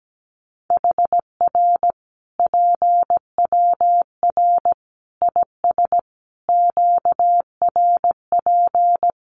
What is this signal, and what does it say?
Morse code, heard as a keyed tone, about 17 wpm, 700 Hz. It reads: HR PWR IS QRP
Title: Common QSO Phrases 17wpm